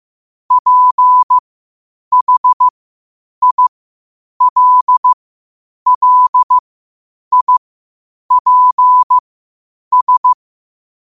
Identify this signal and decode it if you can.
Morse code, heard as a keyed tone, about 15 wpm, spaced as for 8 wpm, 1000 Hz. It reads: PHILLIPS
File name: qotdfiles / 8wpm